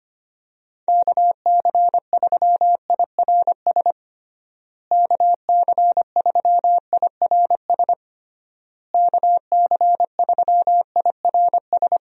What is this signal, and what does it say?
Morse code, heard as a keyed tone, about 25 wpm, 700 Hz. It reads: KC3IRS KC3IRS KC3IRS
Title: USA Call Signs - 6 Characters - Repeated 3x - 25wpm